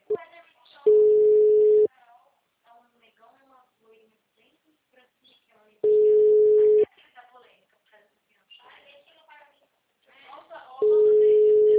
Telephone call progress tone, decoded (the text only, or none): ringback tone